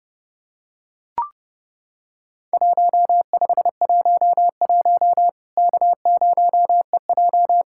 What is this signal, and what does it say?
Morse code, heard as a keyed tone, about 30 wpm, 700 Hz. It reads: 1511 K0EJ
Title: Worked All Europe DX Contest - QTC Exchange - 30wpm